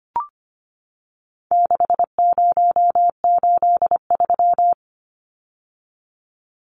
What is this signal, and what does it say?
Morse code, heard as a keyed tone, about 25 wpm, 700 Hz. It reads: 6083
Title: Numbers - 4-Digit 25wpm